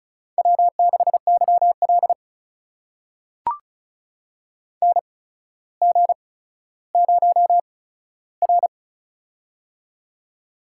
Morse code, heard as a keyed tone, about 36 wpm, spaced as for 15 wpm, 700 Hz. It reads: W6YL NG0R
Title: USA Call Signs - 4 Characters - 8x Character Spacing - 35wpm